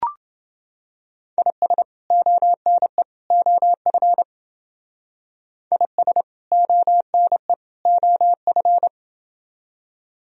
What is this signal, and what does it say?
Morse code, heard as a keyed tone, about 29 wpm, 700 Hz. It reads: IS ONE OF IS ONE OF